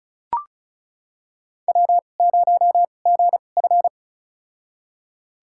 Morse code, heard as a keyed tone, about 36 wpm, spaced as for 25 wpm, 700 Hz. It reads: W0GF
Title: USA Call Signs - 4 Characters - 2x Character Spacing - 35wpm